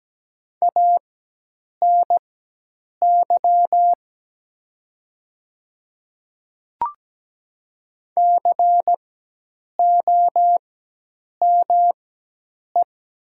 Morse code, heard as a keyed tone, about 17 wpm, spaced as for 8 wpm, 700 Hz. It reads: ANY COME
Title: 4x Character Spacing - Top 300 Words - 17wpm